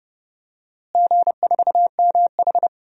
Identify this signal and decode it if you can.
Morse code, heard as a keyed tone, about 30 wpm, 700 Hz. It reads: G4MH
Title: DXCC-100 Call Signs - 4 Characters - Repeated 3x - 30wpm